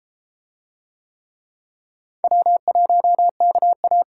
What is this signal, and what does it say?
Morse code, heard as a keyed tone, about 32 wpm, 700 Hz. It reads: W1KA